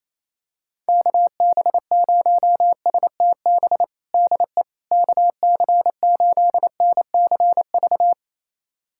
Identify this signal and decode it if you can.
Morse code, heard as a keyed tone, about 27 wpm, 700 Hz. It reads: KB0STB DE KC8NCV
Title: USA Callsign DE Callsign - 6 Characters - 28wpm